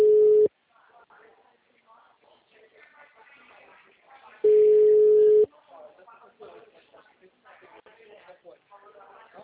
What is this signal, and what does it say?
Ringback tone (Europe, 425 Hz), with speech.